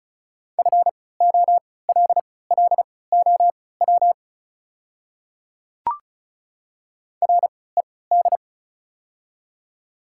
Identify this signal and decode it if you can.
Morse code, heard as a keyed tone, about 36 wpm, spaced as for 20 wpm, 700 Hz. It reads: FOLLOW RED
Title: 3x Character Spacing - Top 500 Words - 35wpm